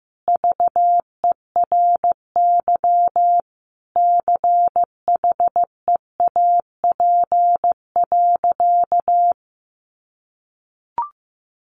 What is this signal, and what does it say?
Morse code, heard as a keyed tone, about 14 wpm, 700 Hz. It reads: VERY CHEAP.
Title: Sentences from Top 1200 Words 15wpm